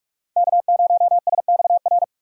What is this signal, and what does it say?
Morse code, heard as a keyed tone, about 42 wpm, 700 Hz. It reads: K0SXR
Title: USA Call Signs - 4x Character Spacing - 45wpm